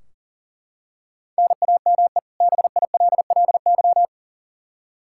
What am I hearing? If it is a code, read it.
Morse code, heard as a keyed tone, about 40 wpm, 700 Hz. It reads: NAME BILLY
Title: Practice QSO - Exchange 1 - Beginning - 40wpm